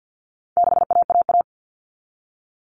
Morse code, heard as a keyed tone, about 50 wpm, 700 Hz. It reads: 6AAA
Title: Common Cut Numbers - 4-Digit 50wpm